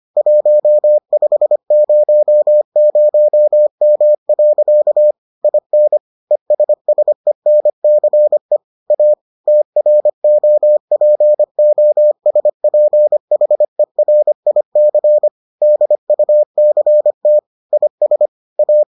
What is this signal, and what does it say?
Morse code, heard as a keyed tone, about 25 wpm, 600 Hz. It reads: 1500M. IN ESSENCE A TROPOSPHERIC DUCT IS A